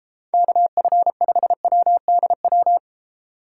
Morse code, heard as a keyed tone, about 29 wpm, 700 Hz. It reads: KF5WDW